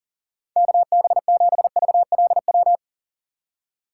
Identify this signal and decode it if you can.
Morse code, heard as a keyed tone, about 40 wpm, 700 Hz. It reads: KB7VLW
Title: USA Call Signs - 6 Characters - 4x Character Spacing - 40wpm